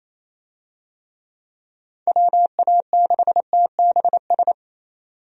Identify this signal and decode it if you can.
Morse code, heard as a keyed tone, about 28 wpm, 700 Hz. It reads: WA6TBS
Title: USA Call Signs - 6 Characters - 28wpm